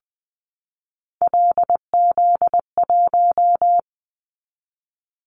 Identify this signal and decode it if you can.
Morse code, heard as a keyed tone, about 20 wpm, 700 Hz. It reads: LZ1
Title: Callsign Prefixes for Top 25 DXCC - 20wpm